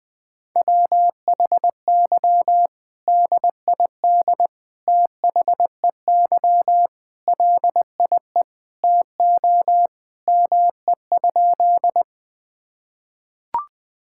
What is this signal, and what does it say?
Morse code, heard as a keyed tone, about 20 wpm, 700 Hz. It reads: WHY DID THEY LIE TO ME?
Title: Sentences from Top 1600 Words 20wpm